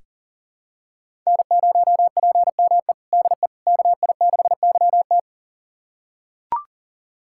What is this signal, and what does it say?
Morse code, heard as a keyed tone, about 40 wpm, 700 Hz. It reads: N0PME DE KI6YT
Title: USA Callsign DE Callsign - 5 Characters - 40wpm